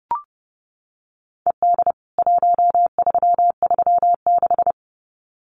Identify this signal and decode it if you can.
Morse code, heard as a keyed tone, about 30 wpm, 700 Hz. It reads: ED 1336
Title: CWT Exchange 30wpm